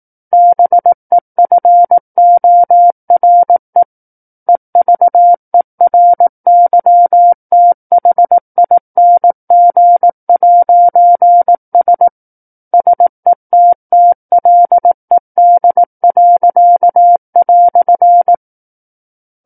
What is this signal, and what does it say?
Morse code, heard as a keyed tone, about 18 wpm, 700 Hz. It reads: BEFORE EVERYTHING'S SETTLED."
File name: 18 WPM